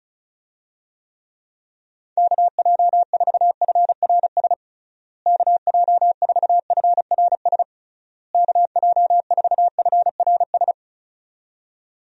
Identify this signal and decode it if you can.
Morse code, heard as a keyed tone, about 34 wpm, 700 Hz. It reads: KJ4FRS KJ4FRS KJ4FRS
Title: USA Call Signs - 6 Characters - Repeated 3x - 35wpm